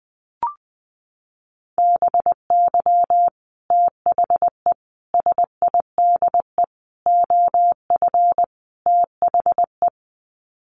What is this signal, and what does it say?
Morse code, heard as a keyed tone, about 20 wpm, 700 Hz. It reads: BY THE SIDE OF THE
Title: Sets of 5 Words Encore2 20wpm